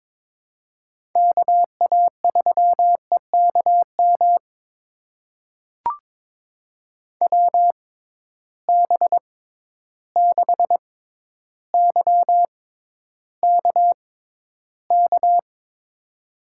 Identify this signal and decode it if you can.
Morse code, heard as a keyed tone, about 22 wpm, spaced as for 11 wpm, 700 Hz. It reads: KA3EKM WB6YKK
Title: USA Call Signs - 6 Characters - 6x Character Spacing - 22wpm